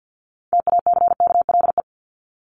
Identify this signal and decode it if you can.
Morse code, heard as a keyed tone, about 50 wpm, 700 Hz. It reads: TACKLE
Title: Top 1751-2000 Words 50wpm